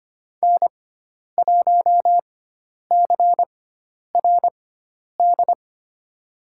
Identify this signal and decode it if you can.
Morse code, heard as a keyed tone, about 25 wpm, spaced as for 10 wpm, 700 Hz. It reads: N1CRD
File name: Lesson 16g - Letter c - Call Signs 25wpm